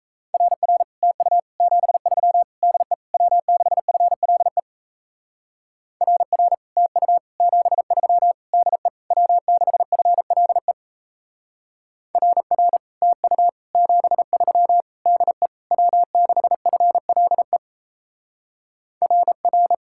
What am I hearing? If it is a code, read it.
Morse code, heard as a keyed tone, about 35 wpm, 700 Hz. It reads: RR TU 73 DE W6FLE RR TU 73 DE W6FLE RR TU 73 DE W6FLE RR
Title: Speed-Racing Common QSO Phrases Encore - 28wpm